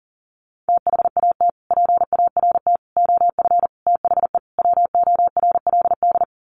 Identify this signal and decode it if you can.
Morse code, heard as a keyed tone, about 40 wpm, 700 Hz. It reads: THAT PART OF THE WORLD
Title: Sets of 5 Words Encore2 40wpm